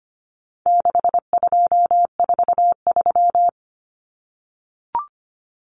Morse code, heard as a keyed tone, about 25 wpm, 700 Hz. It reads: 6243
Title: Numbers - 4-Digit 25wpm